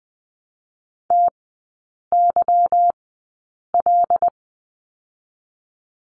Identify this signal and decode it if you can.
Morse code, heard as a keyed tone, about 20 wpm, spaced as for 8 wpm, 700 Hz. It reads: TYL
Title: 166 Lesson 27e - Number 7 - Three Characters 20wpm